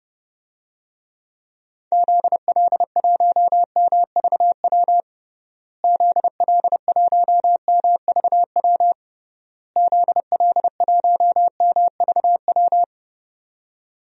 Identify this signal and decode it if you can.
Morse code, heard as a keyed tone, about 32 wpm, 700 Hz. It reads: ZL1MVW ZL1MVW ZL1MVW
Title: DXCC-100 Call Signs - 6 Characters - Repeated 3x - 30wpm